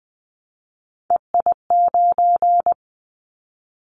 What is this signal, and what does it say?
Morse code, heard as a keyed tone, about 20 wpm, 700 Hz. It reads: EI9
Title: Callsign Prefixes for Top 50 DXCC - 20wpm